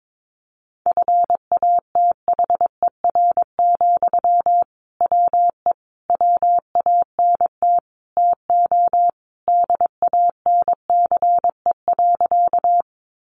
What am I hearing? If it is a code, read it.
Morse code, heard as a keyed tone, about 22 wpm, 700 Hz. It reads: FATHER, WE WANT TO DANCE.
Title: Sentences from Top 700 Words 22wpm